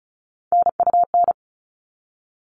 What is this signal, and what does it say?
Morse code, heard as a keyed tone, about 34 wpm, 700 Hz. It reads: NUN
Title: All Cut Numbers - 3-Digit 35wpm